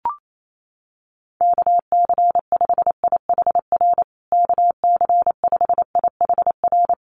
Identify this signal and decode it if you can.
Morse code, heard as a keyed tone, about 28 wpm, 700 Hz. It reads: KC5IHR KC5IHR
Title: USA Call Signs Repeated 3x 28wpm